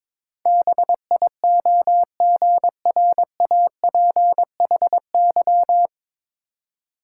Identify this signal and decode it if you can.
Morse code, heard as a keyed tone, about 21 wpm, 700 Hz. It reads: BIOGRAPHY
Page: Morse Code Ninja practice 22wpm